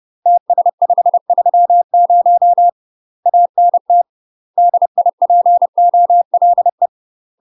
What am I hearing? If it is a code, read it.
Morse code, heard as a keyed tone, about 30 wpm, 700 Hz. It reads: TS530 ANT DIPOLE
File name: UK MASTER QSOs TRAINING / 30WPM / UK MASTER QSOs 0047